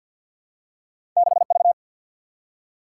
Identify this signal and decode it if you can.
Morse code, heard as a keyed tone, about 53 wpm, 700 Hz. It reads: 6 V
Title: Single-Letter Single-Number 50wpm